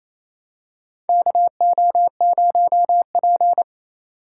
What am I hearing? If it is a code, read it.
Morse code, heard as a keyed tone, about 28 wpm, 700 Hz. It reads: KO0P
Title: USA Call Signs - 4 Characters - 6x Character Spacing - 28wpm